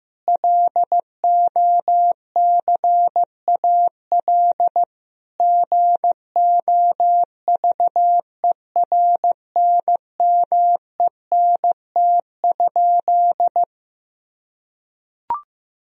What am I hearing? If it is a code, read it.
Morse code, heard as a keyed tone, about 15 wpm, 700 Hz. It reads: LOCAL GOVERNMENT?
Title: Sentences from Top 1000 Words - Review Part 3 - 15wpm